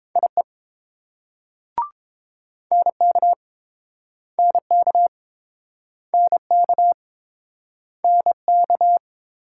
Morse code, heard as a keyed tone, about 30 wpm, 700 Hz. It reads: IE NK NK NK NK